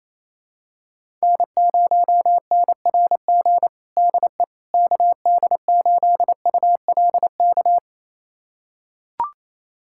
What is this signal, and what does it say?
Morse code, heard as a keyed tone, about 28 wpm, 700 Hz. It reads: N0NRG DE KD8ULK
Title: USA Callsign DE Callsign 28wpm